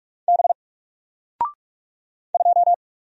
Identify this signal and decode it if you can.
Morse code, heard as a keyed tone, about 44 wpm, 700 Hz. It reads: B 2
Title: Single Letter-Number 45wpm